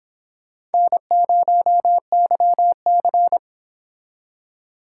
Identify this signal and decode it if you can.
Morse code, heard as a keyed tone, about 29 wpm, 700 Hz. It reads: N0YC